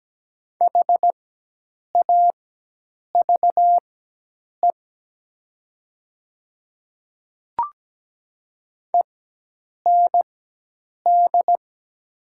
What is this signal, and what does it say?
Morse code, heard as a keyed tone, about 17 wpm, spaced as for 8 wpm, 700 Hz. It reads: HAVE END
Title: 4x Character Spacing - Top 200 Words - 17wpm